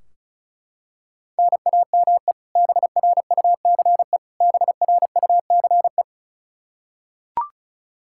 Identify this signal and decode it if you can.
Morse code, heard as a keyed tone, about 36 wpm, 700 Hz. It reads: NAME BRUCE BRUCE
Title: Practice QSO - Exchange 1 - Beginning - 35wpm